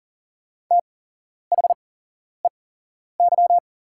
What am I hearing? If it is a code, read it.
Morse code, heard as a keyed tone, about 40 wpm, spaced as for 11 wpm, 700 Hz. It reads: THEY
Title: 8x Character Spacing - Top 200 Words - 40wpm